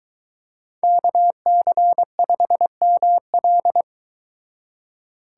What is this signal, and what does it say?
Morse code, heard as a keyed tone, about 23 wpm, 700 Hz. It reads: KC5ML